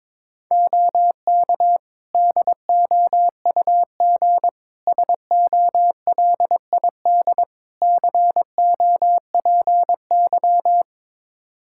Morse code, heard as a keyed tone, about 22 wpm, 700 Hz. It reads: OK DOUG SOLID COPY
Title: Practice QSO - Exchange 3 - More Info - 22wpm